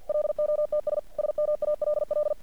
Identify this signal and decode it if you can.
Morse code, heard as a keyed tone, about 50 wpm, 600 Hz. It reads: LOTS SMALL
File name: archived / CRSnet